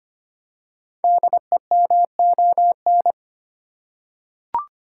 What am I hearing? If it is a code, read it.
Morse code, heard as a keyed tone, about 25 wpm, 700 Hz. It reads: DEMON
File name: 5 Letter Words - First 3 Letters Given - 25wpm